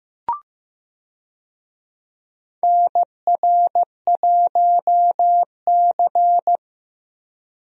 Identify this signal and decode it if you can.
Morse code, heard as a keyed tone, about 15 wpm, 700 Hz. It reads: NR1C
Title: USA Call Signs - 4 Characters - Repeated 3x - 15wpm